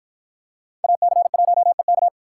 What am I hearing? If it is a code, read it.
Morse code, heard as a keyed tone, about 48 wpm, 700 Hz. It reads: AC1EK